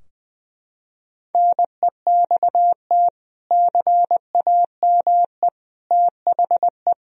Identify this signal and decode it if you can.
Morse code, heard as a keyed tone, about 20 wpm, 700 Hz. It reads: NEXT CAME THE
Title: Sentences from Top 1000 Words - Review Part 4 - 20wpm